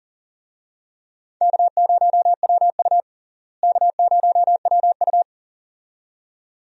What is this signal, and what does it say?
Morse code, heard as a keyed tone, about 40 wpm, 700 Hz. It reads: K0WU K0WU
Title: USA Call Signs - 4 Characters - Repeated 2x - 40wpm